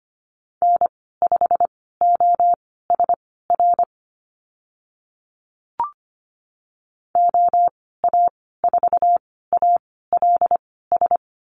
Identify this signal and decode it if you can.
Morse code, heard as a keyed tone, about 25 wpm, spaced as for 16 wpm, 700 Hz. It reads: N5OSR OA4ALS
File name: Lesson 15g - Number 5 - Call Signs 25wpm